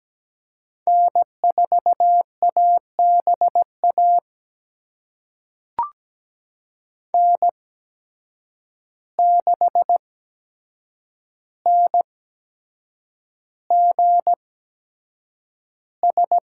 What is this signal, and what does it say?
Morse code, heard as a keyed tone, about 17 wpm, spaced as for 7 wpm, 700 Hz. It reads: N4ABA N6NGS
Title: USA Call Signs - 5 Characters - 8x Character Spacing - 17wpm